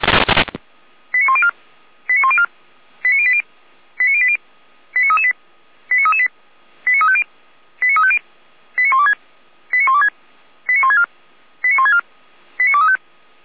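Five-tone selective call (ZVEI), heard as quick runs of five tones, tones of about 70 ms. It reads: 89104 89104 89080 89080 89338 89338 89350 89350 89126 89126 89164 89164 89235
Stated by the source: ZVEI five-tone call received by SDR, server of Feuerwehr Orlamünde fire brigade.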